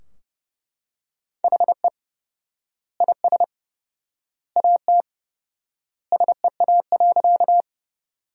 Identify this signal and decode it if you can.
Morse code, heard as a keyed tone, about 31 wpm, 700 Hz. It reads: HE IS AT SEA.
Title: 4x Word-Spacing - Sentences from Top 300 Words - 30wpm